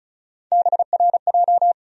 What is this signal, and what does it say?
Morse code, heard as a keyed tone, about 34 wpm, 700 Hz. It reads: BRJ
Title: USA Call Sign Postfixes - 3 Letters 35wpm